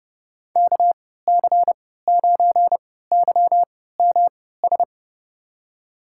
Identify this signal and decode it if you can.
Morse code, heard as a keyed tone, about 30 wpm, spaced as for 17 wpm, 700 Hz. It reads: KC9YMS
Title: USA Call Signs - 6 Characters - 3x Character Spacing - 30wpm